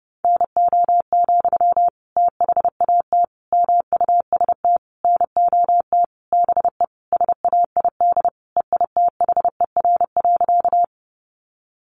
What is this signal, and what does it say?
Morse code, heard as a keyed tone, about 30 wpm, 700 Hz. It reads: NO, THAT MUST NOT BE SAID EITHER.
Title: Sentences from Top 600 Words 30wpm